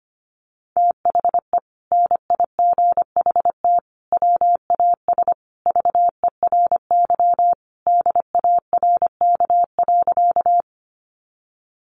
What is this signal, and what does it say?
Morse code, heard as a keyed tone, about 25 wpm, 700 Hz. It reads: THE NIGHT WAS VERY DARK.
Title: Sentences from Top 500 Words 25wpm